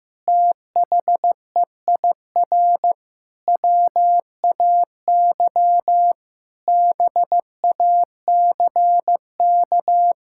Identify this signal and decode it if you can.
Morse code, heard as a keyed tone, about 16 wpm, 700 Hz. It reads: THEIR WAY BACK
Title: Sentences from Top 1000 Words - Review Part 6 - 15wpm